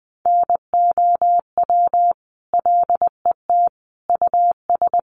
Morse code, heard as a keyed tone, about 20 wpm, 700 Hz. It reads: NOW LET US
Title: Sentences from Top 800 Words 20wpm